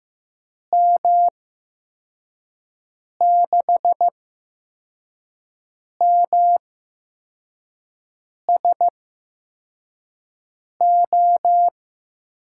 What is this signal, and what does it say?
Morse code, heard as a keyed tone, about 15 wpm, spaced as for 4 wpm, 700 Hz. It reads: M6MSO
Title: USA Call Signs - 6 Characters - 8x Character Spacing - 15wpm